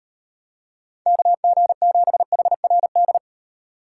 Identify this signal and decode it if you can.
Morse code, heard as a keyed tone, about 38 wpm, 700 Hz. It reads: KG7HRD